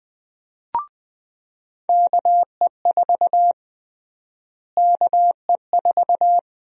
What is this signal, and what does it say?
Morse code, heard as a keyed tone, about 20 wpm, 700 Hz. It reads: KE4 KE4
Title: USA Call Sign Prefixes - Repeated 2x - 20wpm